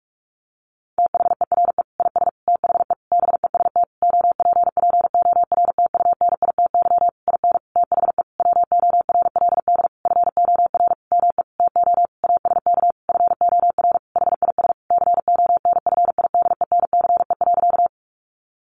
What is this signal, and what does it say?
Morse code, heard as a keyed tone, about 44 wpm, 700 Hz. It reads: THERE IS THE BEST OPPORTUNITY IN THE WORLD FOR ME TO ASK FOR HIS CONFIDENCE.
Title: Sentences from Top 1500 Words 45wpm